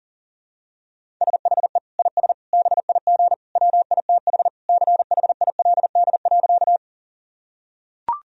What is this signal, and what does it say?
Morse code, heard as a keyed tone, about 37 wpm, 700 Hz. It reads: SHE IS BIG WITH CHILD.
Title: Sentences from Top 1000 Words - Review Part 1 - 40wpm